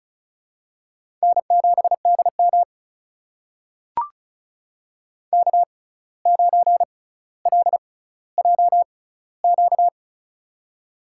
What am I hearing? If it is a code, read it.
Morse code, heard as a keyed tone, about 36 wpm, spaced as for 17 wpm, 700 Hz. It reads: N7DM K9LJQ
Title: USA Call Signs - 6x Character Spacing - 35wpm